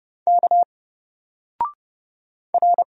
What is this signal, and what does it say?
Morse code, heard as a keyed tone, about 30 wpm, 700 Hz. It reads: K R
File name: Single-Letter-Number-30wpm